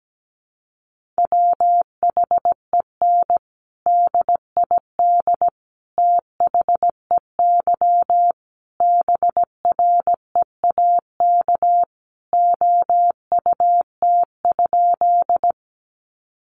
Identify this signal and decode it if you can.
Morse code, heard as a keyed tone, about 17 wpm, 700 Hz. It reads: WHEN DID THEY BREAK OUT?
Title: Sentences from Top 900 Words 17wpm